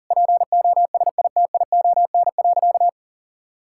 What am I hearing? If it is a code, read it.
Morse code, heard as a keyed tone, about 37 wpm, 700 Hz. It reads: POSITION.
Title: Sentences from Top 1000 Words - Review Part 5 - 40wpm